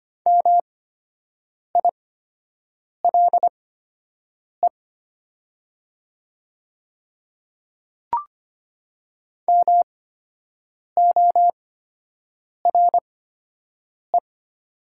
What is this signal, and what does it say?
Morse code, heard as a keyed tone, about 25 wpm, spaced as for 7 wpm, 700 Hz. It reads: MILE MORE
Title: 8x Character Spacing - Top 300 Words - 25wpm